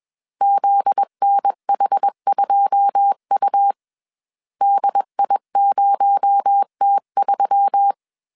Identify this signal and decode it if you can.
Morse code, heard as a keyed tone, about 21 wpm, 800 Hz. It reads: ZNH2U DI0T3